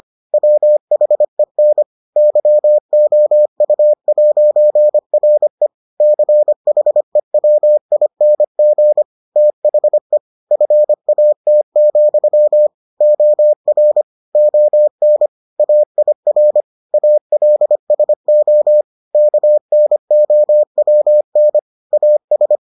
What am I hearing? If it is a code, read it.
Morse code, heard as a keyed tone, about 25 wpm, 600 Hz. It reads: WHEN YOU'RE CHEWING THE FAT, OR ON AIR ALSO KNOWN AS